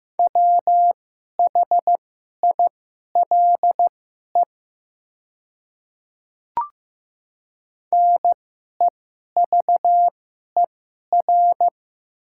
Morse code, heard as a keyed tone, about 15 wpm, spaced as for 11 wpm, 700 Hz. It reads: WHILE NEVER
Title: 2x Character Spacing - Top 300 Words - 15wpm